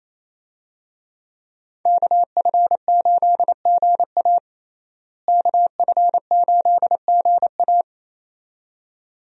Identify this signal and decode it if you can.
Morse code, heard as a keyed tone, about 28 wpm, 700 Hz. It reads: KF8GA KF8GA
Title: USA Call Signs - 5 Characters - Repeated 2x - 28wpm